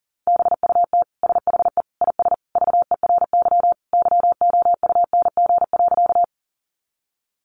Morse code, heard as a keyed tone, about 40 wpm, 700 Hz. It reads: BUT SHE IS VERY YOUNG.
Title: Sentences from Top 400 Words 40wpm